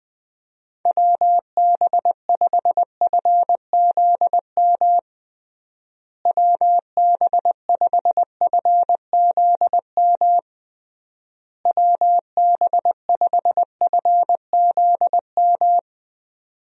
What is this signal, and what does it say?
Morse code, heard as a keyed tone, about 21 wpm, 700 Hz. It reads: WB5FZM WB5FZM WB5FZM
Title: USA Call Signs - 6 Characters - Repeated 3x - 20wpm